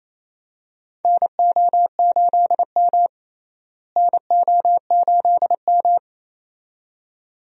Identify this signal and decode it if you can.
Morse code, heard as a keyed tone, about 28 wpm, 700 Hz. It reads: NO8M NO8M
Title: USA Call Signs - 4 Characters - Repeated 2x - 28wpm